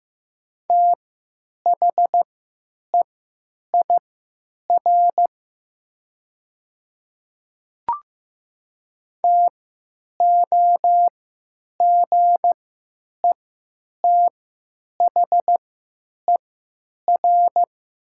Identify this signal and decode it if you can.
Morse code, heard as a keyed tone, about 15 wpm, spaced as for 9 wpm, 700 Hz. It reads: THEIR TOGETHER
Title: 3x Character Spacing - Top 500 Words - 15wpm